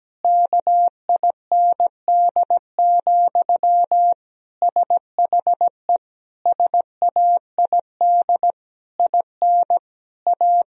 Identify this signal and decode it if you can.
Morse code, heard as a keyed tone, about 17 wpm, 700 Hz. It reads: KIND, SHE SAID IN A
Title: Sentences from Top 1000 Words - Review Part 1 - 17wpm